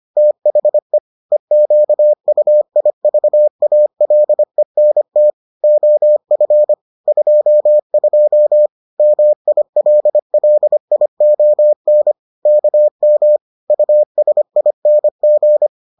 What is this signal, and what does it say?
Morse code, heard as a keyed tone, about 25 wpm, 600 Hz. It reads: THE EQUIVALENT OF 22 MILLION KM USING